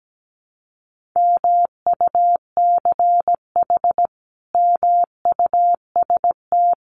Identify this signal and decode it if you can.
Morse code, heard as a keyed tone, about 17 wpm, 700 Hz. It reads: MUCH MUST
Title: Sentences from Top 1400 Words 17wpm